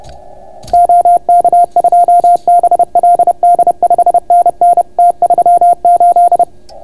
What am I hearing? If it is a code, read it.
Morse code, heard as a keyed tone, about 30 wpm, 690 Hz. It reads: OK2BLD 5NN T38